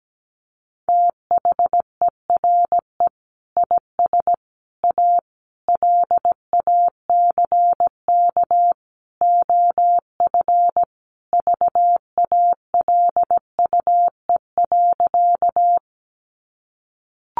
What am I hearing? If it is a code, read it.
Morse code, heard as a keyed tone, about 17 wpm, 700 Hz. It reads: THERE IS A LACK OF VALUE.
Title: Sentences from Top 1100 Words 17wpm